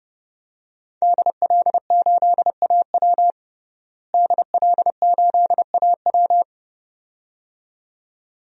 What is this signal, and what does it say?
Morse code, heard as a keyed tone, about 30 wpm, 700 Hz. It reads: DL8AW DL8AW
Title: DXCC-100 Call Signs - 5 Characters - Repeated 2x - 30wpm